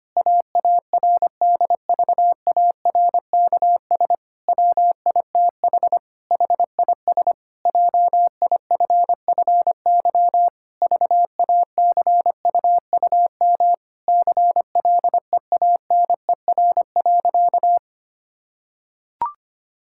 Morse code, heard as a keyed tone, about 25 wpm, 700 Hz. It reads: AARDVARKS WITH HIS JIFFY VACUUM CLEANER.
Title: Pangram Challenge v2 - 25wpm